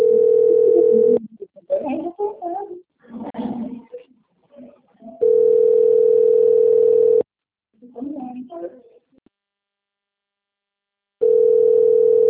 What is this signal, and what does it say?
Ringback tone (North America), 440 + 480 Hz, with speech.
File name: rg-1600-unknown- - - .4786